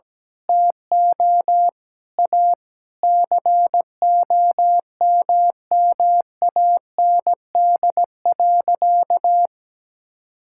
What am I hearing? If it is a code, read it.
Morse code, heard as a keyed tone, about 17 wpm, 700 Hz. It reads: TO A COMMAND.
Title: Sentences from Top 1600 Words 17wpm